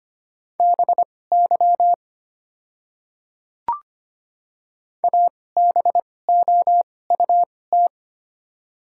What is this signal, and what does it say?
Morse code, heard as a keyed tone, about 25 wpm, spaced as for 18 wpm, 700 Hz. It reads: BY ABOUT
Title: 2x Character Spacing - Top 100 Words - 25wpm